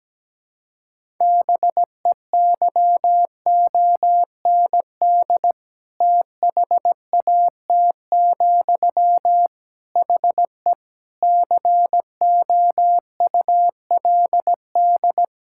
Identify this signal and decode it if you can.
Morse code, heard as a keyed tone, about 17 wpm, 700 Hz. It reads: BEYOND THAT, HE COULD
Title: Sentences from Top 1000 Words - Review Part 6 - 17wpm